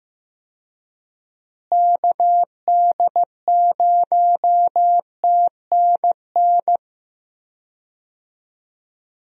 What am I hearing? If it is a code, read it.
Morse code, heard as a keyed tone, about 15 wpm, 700 Hz. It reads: KD0TNN
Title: USA Call Signs - 6 Characters - Repeated 3x - 15wpm